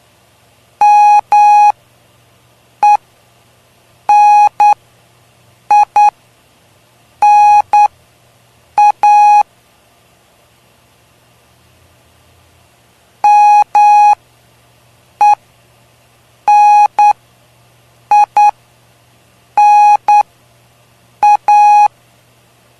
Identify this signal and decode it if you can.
Morse code, heard as a keyed tone, about 9 wpm, spaced as for 6 wpm, 830 Hz. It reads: MENINA MENINA
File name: cw / cwzap / 03-PY2LCD-Li��o 3